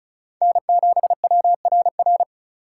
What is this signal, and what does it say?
Morse code, heard as a keyed tone, about 36 wpm, 700 Hz. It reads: N7WRR
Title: USA Call Signs - 4x Character Spacing - 35wpm